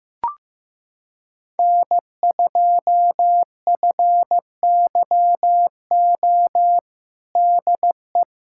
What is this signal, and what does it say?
Morse code, heard as a keyed tone, about 16 wpm, 700 Hz. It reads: N2FYO DE
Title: USA Callsign DE Callsign - 5 Characters - 15wpm